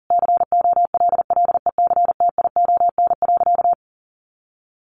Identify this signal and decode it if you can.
Morse code, heard as a keyed tone, about 40 wpm, 700 Hz. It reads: COLLECTION.
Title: Sentences from Top 1200 Words 40wpm